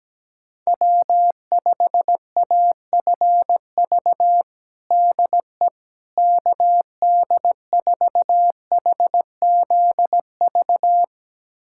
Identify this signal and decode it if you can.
Morse code, heard as a keyed tone, about 17 wpm, 700 Hz. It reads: W5AFV DE KD4HZV
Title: USA Callsign DE Callsign 17wpm